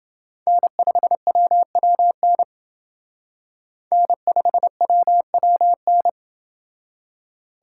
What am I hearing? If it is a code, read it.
Morse code, heard as a keyed tone, about 28 wpm, 700 Hz. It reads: N5WWN N5WWN